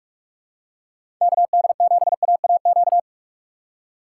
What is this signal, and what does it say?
Morse code, heard as a keyed tone, about 44 wpm, 700 Hz. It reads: KD7AAQ
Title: USA Call Signs - 4x Character Spacing - 45wpm